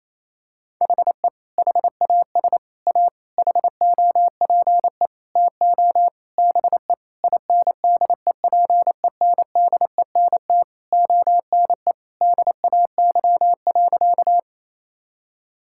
Morse code, heard as a keyed tone, about 28 wpm, 700 Hz. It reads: HE HAS A HOPE TO BE INDEPENDENT ONE DAY.
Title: Sentences from Top 1000 Words - Review Part 6 - 28wpm